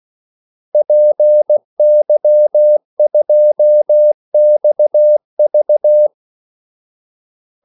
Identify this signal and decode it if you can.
Morse code, heard as a keyed tone, about 16 wpm, 600 Hz. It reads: PY2XV
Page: Kaldesignaler 16 wpm